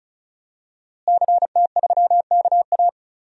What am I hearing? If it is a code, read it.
Morse code, heard as a keyed tone, about 34 wpm, 700 Hz. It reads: CT3KA
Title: DXCC-100 Call Signs - 5 Characters - Repeated 3x - 35wpm